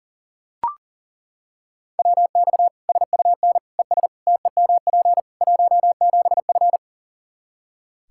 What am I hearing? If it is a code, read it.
Morse code, heard as a keyed tone, about 40 wpm, 700 Hz. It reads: WX SUN ES TEMP 17F
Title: Practice QSO - Exchange 2 - Info - 40wpm